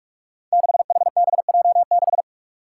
Morse code, heard as a keyed tone, about 46 wpm, 700 Hz. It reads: 6HBJ6
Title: Five Challenging Letter-Number Combinations 45wpm